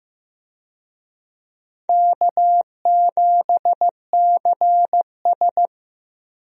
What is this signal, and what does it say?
Morse code, heard as a keyed tone, about 16 wpm, 700 Hz. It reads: K7CS
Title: USA Call Signs - 4 Characters - Repeated 2x - 15wpm